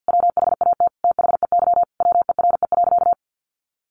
Morse code, heard as a keyed tone, about 50 wpm, 700 Hz. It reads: WHAT THEY WERE.
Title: Sentences from Top 1000 Words 50wpm